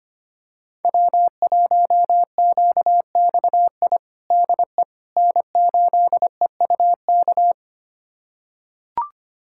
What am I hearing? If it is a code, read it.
Morse code, heard as a keyed tone, about 25 wpm, 700 Hz. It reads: W1QXI DE N8EUK
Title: USA Callsign DE Callsign - 5 Characters - 25wpm